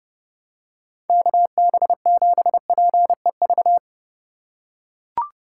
Morse code, heard as a keyed tone, about 28 wpm, 700 Hz. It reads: KB7PEV